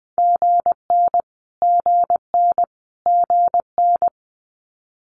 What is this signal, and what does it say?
Morse code, heard as a keyed tone, about 20 wpm, 700 Hz. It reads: GN GN GN
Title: Multi-letter Phonemes Repeated 3x 20wpm